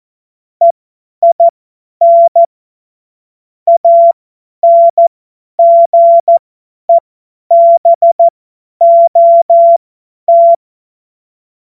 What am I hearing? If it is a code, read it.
Morse code, heard as a keyed tone, about 14 wpm, spaced as for 10 wpm, 680 Hz. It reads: EIN ANGEBOT
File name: Morsekurs / Witze / Witz 27 Februar 2024 50 F70